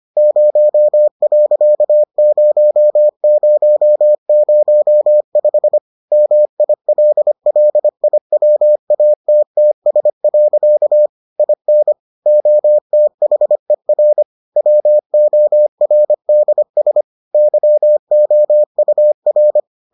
Morse code, heard as a keyed tone, about 25 wpm, 600 Hz. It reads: 0.0005 MILLIWATTS. IN OTHER WORDS YOUR